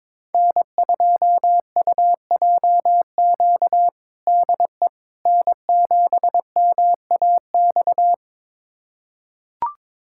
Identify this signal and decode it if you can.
Morse code, heard as a keyed tone, about 22 wpm, 700 Hz. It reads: N2UJQ DE N7MAX
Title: USA Callsign DE Callsign - 5 Characters - 22wpm